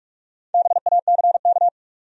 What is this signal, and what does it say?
Morse code, heard as a keyed tone, about 57 wpm, 700 Hz. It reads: BACK